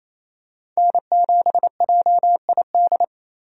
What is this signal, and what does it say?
Morse code, heard as a keyed tone, about 28 wpm, 700 Hz. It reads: N7JID
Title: USA Call Signs - 5 Characters - 28wpm